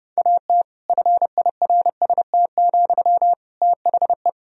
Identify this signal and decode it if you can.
Morse code, heard as a keyed tone, about 30 wpm, 700 Hz. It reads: AT FIRST, THE
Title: Sentences from Top 1000 Words - Review Part 4 - 30wpm